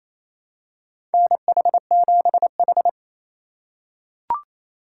Morse code, heard as a keyed tone, about 28 wpm, 700 Hz. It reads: NH7H
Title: USA Call Signs - 4 Characters - 6x Character Spacing - 28wpm